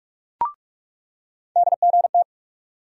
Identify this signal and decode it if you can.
Morse code, heard as a keyed tone, about 42 wpm, 700 Hz. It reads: DGT
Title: USA Call Sign Postfixes - 3 Letters 45wpm